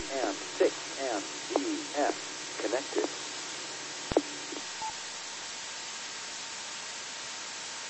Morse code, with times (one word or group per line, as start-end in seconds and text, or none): none